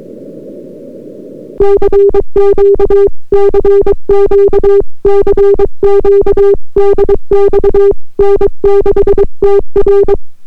Morse code, heard as a keyed tone, about 23 wpm, 400 Hz. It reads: CQ CQ CQ DX N6TR